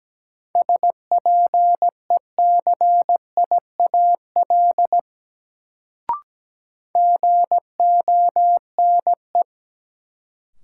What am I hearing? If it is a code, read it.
Morse code, heard as a keyed tone, about 17 wpm, 700 Hz. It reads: SPECIAL GONE
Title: Sentences from Top 1200 Words - New Word Drill Alt - 17wpm